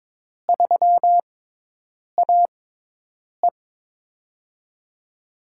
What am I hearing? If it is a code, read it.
Morse code, heard as a keyed tone, about 22 wpm, spaced as for 8 wpm, 700 Hz. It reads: 3AE
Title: USA Call Signs - 4 Characters - 6x Character Spacing - 22wpm